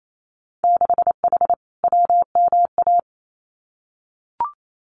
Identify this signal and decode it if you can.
Morse code, heard as a keyed tone, about 28 wpm, 700 Hz. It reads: 6H WMA
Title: Winter Field Day Exchanges Encore 28wpm